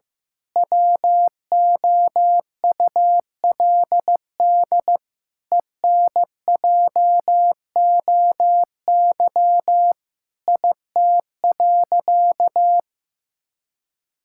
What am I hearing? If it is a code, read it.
Morse code, heard as a keyed tone, about 15 wpm, 700 Hz. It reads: WOULD ENJOY IT.
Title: Sentences from Top 1000 Words - Review Part 5 - 15wpm